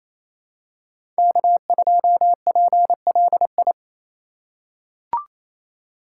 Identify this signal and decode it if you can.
Morse code, heard as a keyed tone, about 28 wpm, 700 Hz. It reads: K2PLI
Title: USA Call Signs - 5 Characters - 2x Character Spacing - 28wpm